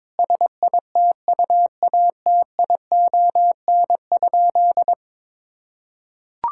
Morse code, heard as a keyed tone, about 22 wpm, 700 Hz. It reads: SITUATION?
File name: practice / 22wpm / Sentences with 8-Word Phrases - 22wpm